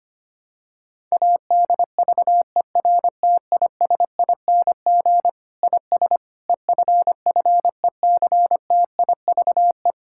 Morse code, heard as a keyed tone, about 25 wpm, 700 Hz. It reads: ADVERTISING IS EFFECTIVE
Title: Sentences from Top 1000 Words - Review Part 6 - 25wpm